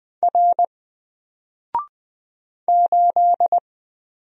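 Morse code, heard as a keyed tone, about 20 wpm, 700 Hz. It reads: R 8
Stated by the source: Single letter and number Morse drill at 20 wpm, from N4LCD.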